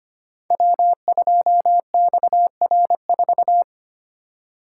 Morse code, heard as a keyed tone, about 25 wpm, 690 Hz. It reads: W2XR4
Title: Code Groups - 5 Letter:Numbers - 25wpm